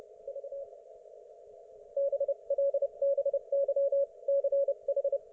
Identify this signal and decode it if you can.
Morse code, heard as a keyed tone, about 30 wpm, spaced as for 25 wpm, 560 Hz. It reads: V BLBYCH
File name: Archiv / CUC208 / ok1fhi cul208